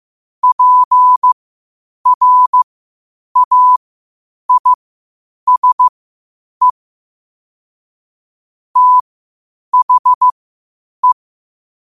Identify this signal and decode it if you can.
Morse code, heard as a keyed tone, about 15 wpm, spaced as for 8 wpm, 1000 Hz. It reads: PRAISE THE